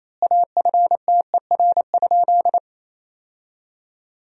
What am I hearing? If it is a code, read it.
Morse code, heard as a keyed tone, about 28 wpm, 700 Hz. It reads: AFTER?
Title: 2x Word-Spacing - Sentences from Top 200 Words - 28wpm